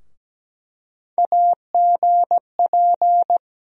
Morse code, heard as a keyed tone, about 17 wpm, 700 Hz. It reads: AGP
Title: USA Call Sign Postfixes - 3 Letters 17wpm